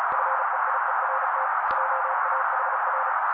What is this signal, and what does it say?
Morse code, heard as a keyed tone, about 35 wpm, 530 Hz. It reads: FEIERT JAHR
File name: daily / 35wpm noise